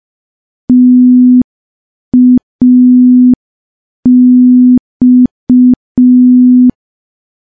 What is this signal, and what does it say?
Morse code, heard as a keyed tone, about 5 wpm, 250 Hz. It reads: TAX
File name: morse / morse-training-05WPM